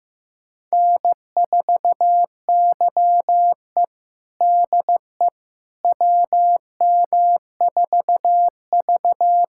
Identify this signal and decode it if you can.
Morse code, heard as a keyed tone, about 15 wpm, 700 Hz. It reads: N4YE DE WM4V
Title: USA Callsign DE Callsign - 4 Characters - 15wpm